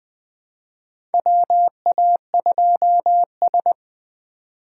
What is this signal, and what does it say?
Morse code, heard as a keyed tone, about 20 wpm, 700 Hz. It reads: WA2S
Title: USA Call Signs - 4 Characters - 2x Character Spacing - 20wpm